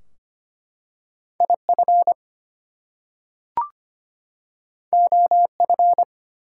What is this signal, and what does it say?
Morse code, heard as a keyed tone, about 25 wpm, 700 Hz. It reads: IF OF
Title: Two Letter Words - 25wpm